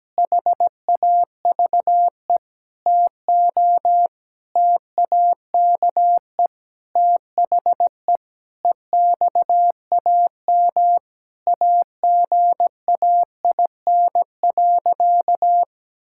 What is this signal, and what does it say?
Morse code, heard as a keyed tone, about 17 wpm, 700 Hz. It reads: HAVE TO TAKE THE EXAM AGAIN.
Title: Sentences from Top 1000 Words - Review Part 5 - 17wpm